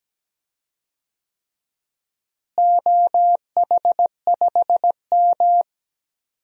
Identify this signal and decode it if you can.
Morse code, heard as a keyed tone, about 17 wpm, 700 Hz. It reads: OH5M
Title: DXCC-100 Call Signs - 4 Characters - Repeated 3x - 17wpm